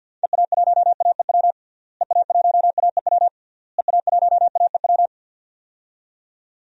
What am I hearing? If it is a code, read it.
Morse code, heard as a keyed tone, about 47 wpm, 700 Hz. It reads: EA1AEW EA1AEW EA1AEW
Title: DXCC-100 Call Signs - 6 Characters - Repeated 3x - 50wpm